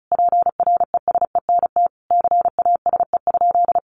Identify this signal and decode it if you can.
Morse code, heard as a keyed tone, about 34 wpm, 700 Hz. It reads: PRESENT CASE?
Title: Sentences from Top 1200 Words 35wpm